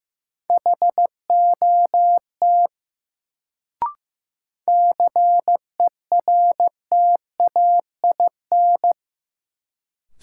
Morse code, heard as a keyed tone, about 15 wpm, 700 Hz. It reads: HOT CERTAIN
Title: Sentences from Top 500 Words - New Word Drill - 15wpm